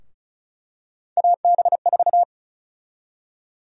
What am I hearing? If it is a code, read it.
Morse code, heard as a keyed tone, about 36 wpm, 700 Hz. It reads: AB4
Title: USA Call Sign Prefixes - Repeated 2x - 35wpm